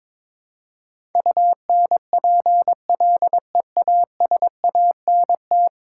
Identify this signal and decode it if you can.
Morse code, heard as a keyed tone, about 22 wpm, 700 Hz. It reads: UNPLEASANT
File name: practice / 22wpm / 10 Letter Words - First 3 Letters Given - 22wpm